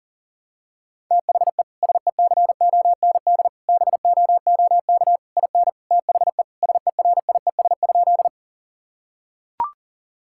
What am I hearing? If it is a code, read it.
Morse code, heard as a keyed tone, about 40 wpm, 700 Hz. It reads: THE SECOND BOOK IN THE SERIES?
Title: Sentences from Top 1000 Words - Review Part 5 - 40wpm